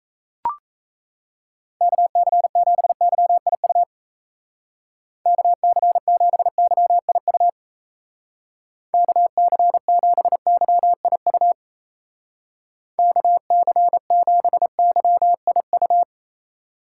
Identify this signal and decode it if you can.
Morse code, heard as a keyed tone, about 34 wpm, 700 Hz. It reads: KC7YIU KC7YIU KC7YIU KC7YIU
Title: Speed Racing - USA 2x3 Call Signs 28wpm